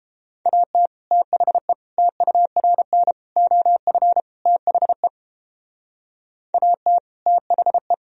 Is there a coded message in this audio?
Yes: Morse code from 0.5 to 7.9 s, heard as a keyed tone, about 30 wpm, 700 Hz. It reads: AT THE TURN OF THE AT THE